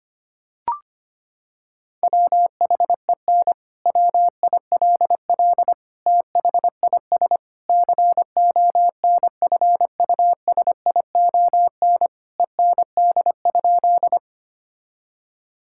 Morse code, heard as a keyed tone, about 25 wpm, 700 Hz. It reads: WHEN WILL THIS CONFUSION END?
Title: Sentences from Top 1600 Words 25wpm